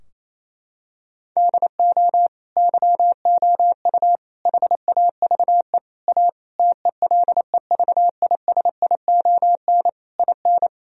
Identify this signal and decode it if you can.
Morse code, heard as a keyed tone, about 27 wpm, 700 Hz. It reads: DO YOU HAVE A TELEVISION IN
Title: Sentences from Top 1000 Words - Review Part 4 - 28wpm